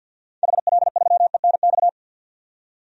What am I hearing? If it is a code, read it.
Morse code, heard as a keyed tone, about 46 wpm, 700 Hz. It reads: HL3ENX
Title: DXCC-100 Call Signs - 6 Characters - 50wpm